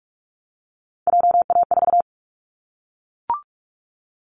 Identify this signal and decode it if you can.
Morse code, heard as a keyed tone, about 44 wpm, 700 Hz. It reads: JA4
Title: Callsign Prefixes for Top 25 DXCC - 45wpm